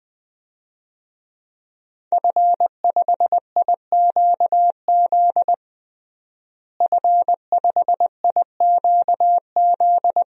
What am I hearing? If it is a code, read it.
Morse code, heard as a keyed tone, about 20 wpm, 700 Hz. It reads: F5IQZ F5IQZ
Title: DXCC-100 Call Signs - 5 Characters - Repeated 2x - 20wpm